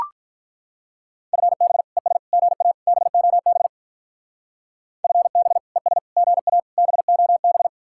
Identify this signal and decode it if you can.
Morse code, heard as a keyed tone, about 43 wpm, 700 Hz. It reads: FB ES GA BOB FB ES GA BOB
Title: Speed-Racing Common QSO Phrases Encore - 35wpm